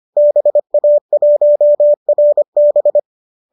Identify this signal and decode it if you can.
Morse code, heard as a keyed tone, about 25 wpm, 590 Hz. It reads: BA1RB
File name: qso203 25